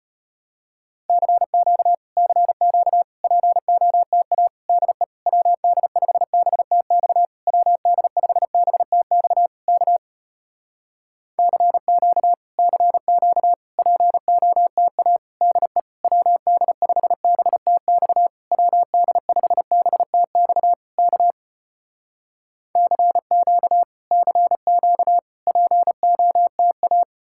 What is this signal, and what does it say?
Morse code, heard as a keyed tone, about 35 wpm, 700 Hz. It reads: CQ CQ POTA DE WD5BTX WD5BTX K CQ CQ POTA DE WD5BTX WD5BTX K CQ CQ POTA